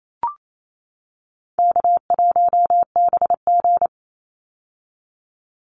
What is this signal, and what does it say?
Morse code, heard as a keyed tone, about 28 wpm, 700 Hz. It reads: K1BG
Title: Easy Call Signs - 28wpm